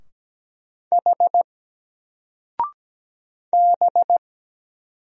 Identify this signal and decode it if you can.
Morse code, heard as a keyed tone, about 17 wpm, 700 Hz. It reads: H B
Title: Single Letter-Number 17wpm